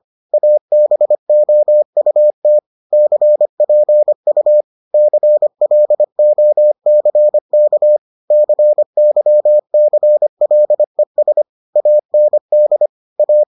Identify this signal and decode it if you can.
Morse code, heard as a keyed tone, about 25 wpm, 600 Hz. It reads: ABOUT CPU CLOCK CYCLES AND A